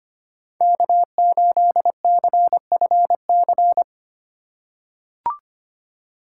Morse code, heard as a keyed tone, about 25 wpm, 700 Hz. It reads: K8CFC
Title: USA Call Signs - 5 Characters - 4x Character Spacing - 25wpm